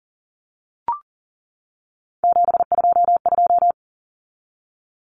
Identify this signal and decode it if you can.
Morse code, heard as a keyed tone, about 40 wpm, 700 Hz. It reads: 722
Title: Numbers - 3-Digit 40wpm